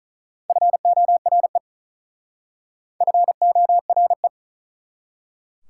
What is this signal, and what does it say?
Morse code, heard as a keyed tone, about 43 wpm, 690 Hz. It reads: FORE FORE
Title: Speed-Racing Prefixes and Suffixes 35wpm